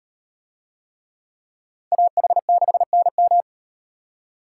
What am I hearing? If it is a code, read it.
Morse code, heard as a keyed tone, about 38 wpm, 700 Hz. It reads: AH6NM